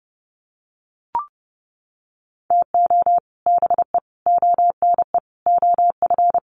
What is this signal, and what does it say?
Morse code, heard as a keyed tone, about 30 wpm, 700 Hz. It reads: TO BE ONE OF
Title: Sets of 4 Words 30wpm